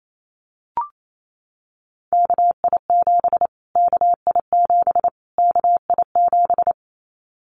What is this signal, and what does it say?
Morse code, heard as a keyed tone, about 28 wpm, 700 Hz. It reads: KI7 KI7 KI7
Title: Callsign Prefixes for Top 25 DXCC - Repeated 3x - 28wpm